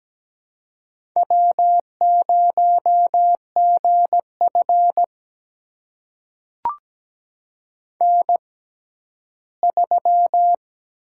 Morse code, heard as a keyed tone, about 17 wpm, spaced as for 11 wpm, 700 Hz. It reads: W0GF N3
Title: USA Call Signs - 4 Characters - 6x Character Spacing - 17wpm